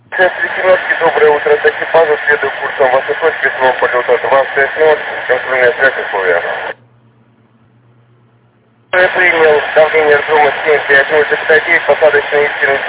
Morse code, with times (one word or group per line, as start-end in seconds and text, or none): none